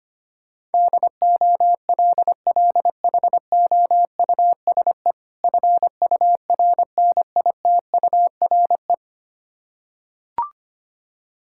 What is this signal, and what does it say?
Morse code, heard as a keyed tone, about 25 wpm, 700 Hz. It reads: DOLLHOUSE FURNITURE
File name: practice / 25wpm / Difficult Words with Hints - 25wpm